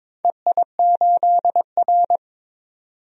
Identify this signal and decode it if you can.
Morse code, heard as a keyed tone, about 21 wpm, 700 Hz. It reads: EI8R
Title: DXCC-100 Call Signs - 4 Characters - 22wpm